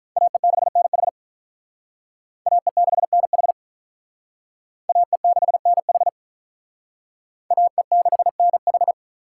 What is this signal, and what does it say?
Morse code, heard as a keyed tone, about 45 wpm, 700 Hz. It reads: AE6NH AE6NH AE6NH AE6NH